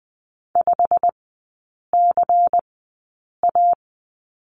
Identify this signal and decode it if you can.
Morse code, heard as a keyed tone, about 20 wpm, spaced as for 8 wpm, 700 Hz. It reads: 5CA
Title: 093 Lesson 16e - Letter c - Three Characters 20wpm